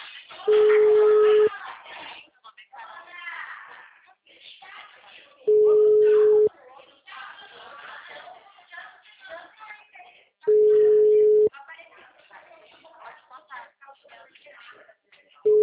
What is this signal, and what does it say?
Ringback tone (Europe, 425 Hz), with speech.
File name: out- -2010- - - .30593